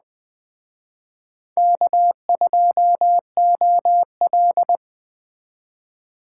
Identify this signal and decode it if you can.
Morse code, heard as a keyed tone, about 21 wpm, 700 Hz. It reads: K2OL
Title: Speed Racing - USA 1x2 Call Signs 17wpm